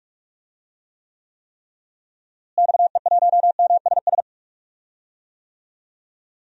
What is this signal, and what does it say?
Morse code, heard as a keyed tone, about 43 wpm, 700 Hz. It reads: XE1MSS
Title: DXCC-100 Call Signs - 6 Characters - Repeated 3x - 45wpm